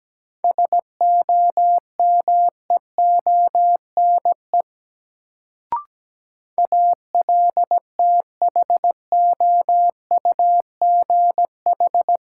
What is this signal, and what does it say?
Morse code, heard as a keyed tone, about 17 wpm, 700 Hz. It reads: SOMEONE ALTHOUGH
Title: Sentences from Top 600 Words - New Word Drill - 17wpm